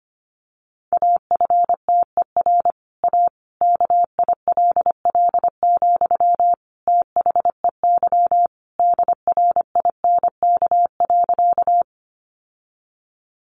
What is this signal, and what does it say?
Morse code, heard as a keyed tone, about 25 wpm, 700 Hz. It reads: AFTER A KILL, THEY DRINK.
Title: Sentences from Top 1500 Words 25wpm